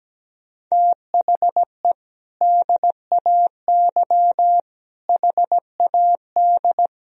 Morse code, heard as a keyed tone, about 17 wpm, 700 Hz. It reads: THE DAY HAD
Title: Sentences from Top 1000 Words - Review Part 3 - 17wpm